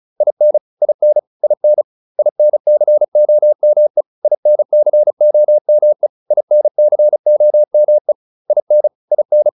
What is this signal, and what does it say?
Morse code, heard as a keyed tone, about 34 wpm, 600 Hz. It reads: IN IN IN INCOME INCOME INCOME IN IN
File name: IN6 35